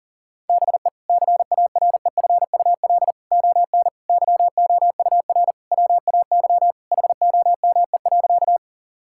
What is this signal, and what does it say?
Morse code, heard as a keyed tone, about 40 wpm, 700 Hz. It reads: BE CAREFUL ON YOUR WAY HOME.
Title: Sentences from Top 1600 Words 40wpm